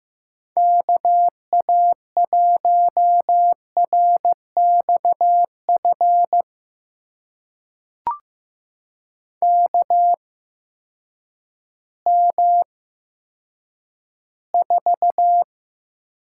Morse code, heard as a keyed tone, about 15 wpm, spaced as for 9 wpm, 700 Hz. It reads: KA1RXF KM4
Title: USA Call Signs - 6 Characters - 8x Character Spacing - 15wpm